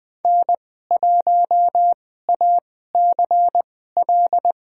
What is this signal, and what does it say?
Morse code, heard as a keyed tone, about 20 wpm, spaced as for 14 wpm, 700 Hz. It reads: N1ACL
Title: USA Call Signs - 5 Characters - 2x Character Spacing - 20wpm